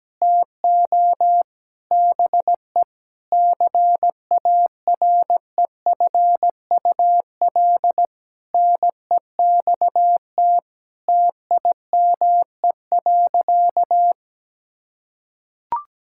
Morse code, heard as a keyed tone, about 17 wpm, 700 Hz. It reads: TO BE CAREFUL NEXT TIME.
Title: Sentences from Top 1600 Words 17wpm